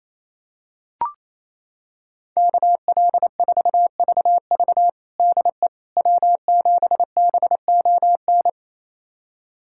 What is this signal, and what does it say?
Morse code, heard as a keyed tone, about 28 wpm, 700 Hz. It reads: KL4VV DE W7BON
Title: USA Callsign DE Callsign - 5 Characters - 28wpm